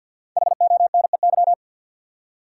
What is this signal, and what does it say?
Morse code, heard as a keyed tone, about 53 wpm, 700 Hz. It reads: HONEY
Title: Five Letter Words - 50wpm